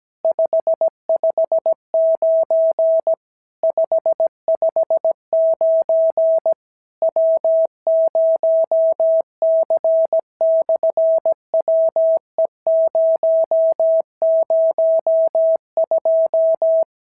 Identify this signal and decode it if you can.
Morse code, heard as a keyed tone, about 17 wpm, 650 Hz. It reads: 559 559 W0C/WE002
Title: SOTA QSO Practice 17wpm